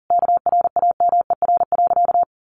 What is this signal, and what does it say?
Morse code, heard as a keyed tone, about 40 wpm, 700 Hz. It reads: KRAMER.
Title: A Sound Of Thunder 40wpm